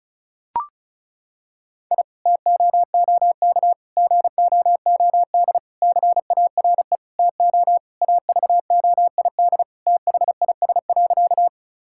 Morse code, heard as a keyed tone, about 36 wpm, 700 Hz. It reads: I TOOK GOOD CARE TO AVOID THIS.
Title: Sentences from Top 1000 Words - Review Part 6 - 35wpm